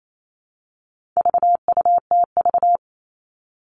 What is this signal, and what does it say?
Morse code, heard as a keyed tone, about 28 wpm, 700 Hz. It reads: VUTV
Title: All Cut Numbers - 4-Digit 28wpm